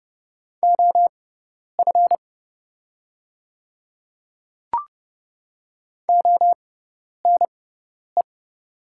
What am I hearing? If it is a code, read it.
Morse code, heard as a keyed tone, about 30 wpm, spaced as for 10 wpm, 700 Hz. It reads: OF ONE
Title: 6x Character Spacing - Top 300 Words - 30wpm